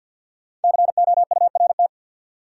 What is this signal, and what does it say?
Morse code, heard as a keyed tone, about 53 wpm, 700 Hz. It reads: COURT
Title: Sentences from Top 1600 Words - New Word Drill - 50wpm